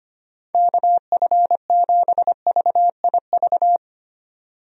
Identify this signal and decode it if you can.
Morse code, heard as a keyed tone, about 25 wpm, 700 Hz. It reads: KF7VIV
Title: USA Call Signs - 6 Characters - 25wpm